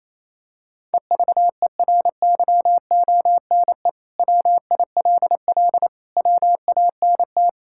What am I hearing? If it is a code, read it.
Morse code, heard as a keyed tone, about 28 wpm, 700 Hz. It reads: EVERYONE WILL WANT
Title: Sentences from Top 1000 Words - Review Part 5 - 28wpm